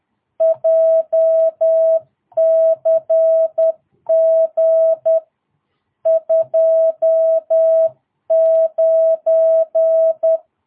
Morse code, heard as a keyed tone, about 10 wpm, 660 Hz. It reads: JCG 29
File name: jh8cbh / 2022cw